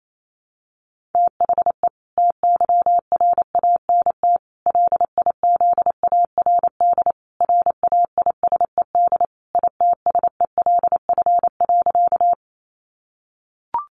Morse code, heard as a keyed tone, about 28 wpm, 700 Hz. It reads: THE TYRANT LIZARD RAISED ITSELF.
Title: A Sound Of Thunder 28wpm